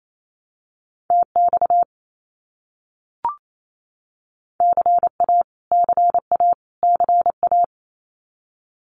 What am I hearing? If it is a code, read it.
Morse code, heard as a keyed tone, about 28 wpm, 700 Hz. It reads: TX CA CA CA
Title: US States Repeated 3x 28wpm